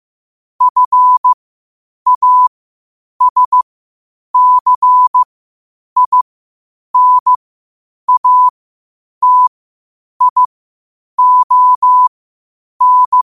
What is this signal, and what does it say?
Morse code, heard as a keyed tone, about 15 wpm, spaced as for 8 wpm, 1000 Hz. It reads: FASCINATION
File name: qotdfiles / 8wpm